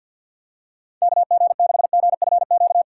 Morse code, heard as a keyed tone, about 54 wpm, 700 Hz. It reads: KG6GFQ
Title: USA Call Signs - 6 Characters - 6x Character Spacing - 50wpm